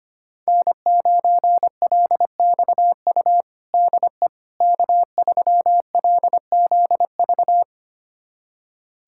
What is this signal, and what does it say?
Morse code, heard as a keyed tone, about 25 wpm, 700 Hz. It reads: N9LXU DE K3LZV
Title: USA Callsign DE Callsign - 5 Characters - 25wpm